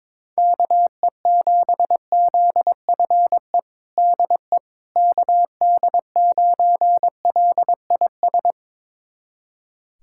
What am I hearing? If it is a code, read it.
Morse code, heard as a keyed tone, about 21 wpm, 700 Hz. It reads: KE7ZFE DE KD9LIS
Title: USA Callsign DE Callsign 22wpm